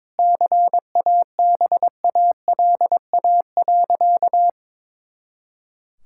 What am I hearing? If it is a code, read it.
Morse code, heard as a keyed tone, about 22 wpm, 700 Hz. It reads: CABALA.
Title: Pangram Challenge v2 - 22wpm